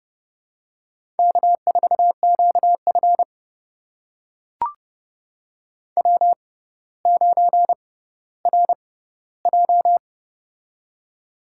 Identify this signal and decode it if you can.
Morse code, heard as a keyed tone, about 32 wpm, spaced as for 16 wpm, 700 Hz. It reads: K4QF W9RJ
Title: USA Call Signs - 4 Characters - 6x Character Spacing - 30wpm